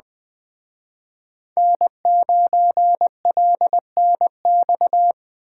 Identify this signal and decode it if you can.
Morse code, heard as a keyed tone, about 24 wpm, 700 Hz. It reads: N9LNX
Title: Speed Racing - USA Call Signs 20wpm